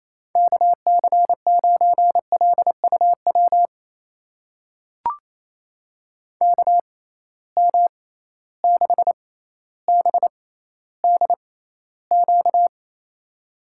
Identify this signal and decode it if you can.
Morse code, heard as a keyed tone, about 28 wpm, spaced as for 14 wpm, 700 Hz. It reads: KC9LUW KM6BDQ
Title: USA Call Signs - 6 Characters - 6x Character Spacing - 28wpm